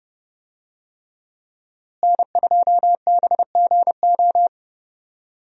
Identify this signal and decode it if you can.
Morse code, heard as a keyed tone, about 30 wpm, 700 Hz. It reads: N2BGO
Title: USA Call Signs - 5 Characters - Repeated 3x - 30wpm